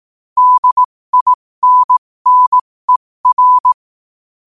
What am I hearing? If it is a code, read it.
Morse code, heard as a keyed tone, about 18 wpm, spaced as for 15 wpm, 1000 Hz. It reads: DINNER
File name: qotdfiles / 15wpm